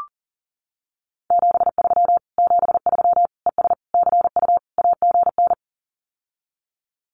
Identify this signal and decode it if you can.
Morse code, heard as a keyed tone, about 40 wpm, 700 Hz. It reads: 73 73 ES CU AGN
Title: Common QSO Phrases Encore 40wpm